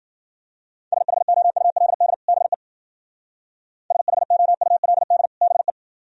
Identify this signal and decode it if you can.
Morse code, heard as a keyed tone, about 52 wpm, 700 Hz. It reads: SHOULD BE SHOULD BE